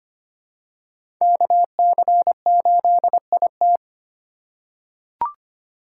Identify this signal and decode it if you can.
Morse code, heard as a keyed tone, about 25 wpm, 700 Hz. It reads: KC8IT
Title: USA Call Signs - 5 Characters - 3x Character Spacing - 25wpm